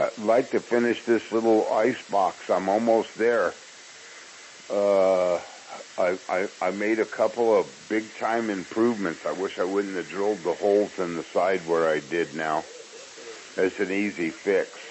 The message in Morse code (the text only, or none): none